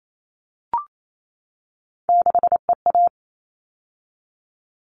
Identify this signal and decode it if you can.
Morse code, heard as a keyed tone, about 28 wpm, 700 Hz. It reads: 6EA
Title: All Cut Numbers - 3-Digit 28wpm